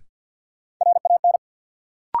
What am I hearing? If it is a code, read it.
Morse code, heard as a keyed tone, about 52 wpm, 700 Hz. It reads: RAN
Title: Sentences from Top 500 Words - New Word Drill - 50wpm